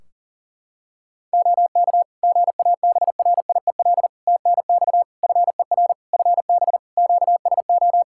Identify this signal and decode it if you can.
Morse code, heard as a keyed tone, about 40 wpm, 700 Hz. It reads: OK GABRIEL TNX FER FB QSO
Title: Practice QSO - Exchange 4 - Ending - 40wpm